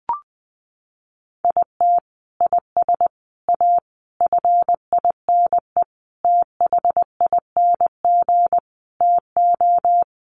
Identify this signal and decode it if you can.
Morse code, heard as a keyed tone, about 20 wpm, 700 Hz. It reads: IT IS A FINE THING TO
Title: Sentences from Top 1000 Words 20wpm